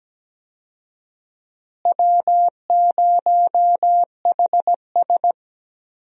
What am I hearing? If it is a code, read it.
Morse code, heard as a keyed tone, about 17 wpm, 700 Hz. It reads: W0HS
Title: USA Call Signs - 4 Characters - Repeated 2x - 17wpm